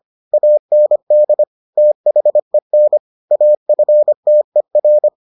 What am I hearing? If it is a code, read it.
Morse code, heard as a keyed tone, about 25 wpm, 600 Hz. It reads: AND THEN AFTER